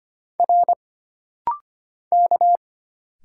Morse code, heard as a keyed tone, about 25 wpm, 700 Hz. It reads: R K
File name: Single Character (Letter:Number:Punctuation:Prosign) - Rapid-Fire - 25wpm